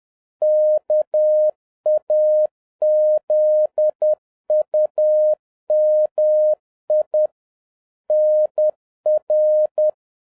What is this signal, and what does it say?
Morse code, heard as a keyed tone, about 10 wpm, 610 Hz. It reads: KAZUMI NR